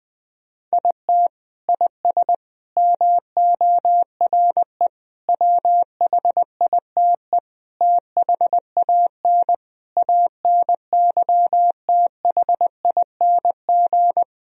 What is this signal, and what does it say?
Morse code, heard as a keyed tone, about 21 wpm, 700 Hz. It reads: IT IS MORE WHITE THAN ANYTHING
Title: Sentences from Top 1000 Words - Review Part 1 - 20wpm